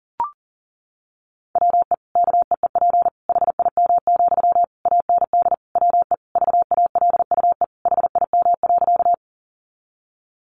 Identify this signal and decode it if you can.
Morse code, heard as a keyed tone, about 40 wpm, 700 Hz. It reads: WE KEEP HIM, AND WE VALUE HIM.
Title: Sentences from Top 600 Words 40wpm